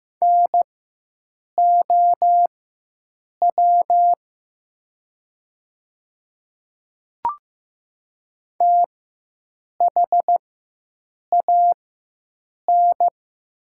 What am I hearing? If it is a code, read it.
Morse code, heard as a keyed tone, about 15 wpm, spaced as for 7 wpm, 700 Hz. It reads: NOW THAN
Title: 4x Character Spacing - Top 200 Words - 15wpm